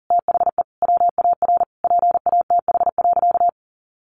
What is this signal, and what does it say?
Morse code, heard as a keyed tone, about 37 wpm, 700 Hz. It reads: THE WAR PATH.
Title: Sentences from Top 1400 Words 40wpm